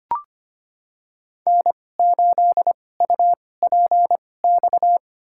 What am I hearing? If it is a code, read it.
Morse code, heard as a keyed tone, about 25 wpm, spaced as for 18 wpm, 700 Hz. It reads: N8UPX
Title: USA Call Signs - 5 Characters - 2x Character Spacing - 25wpm